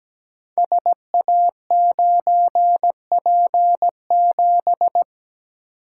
Morse code, heard as a keyed tone, about 17 wpm, 700 Hz. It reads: SA9P7
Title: Code Groups - 5 Letter:Numbers - 17wpm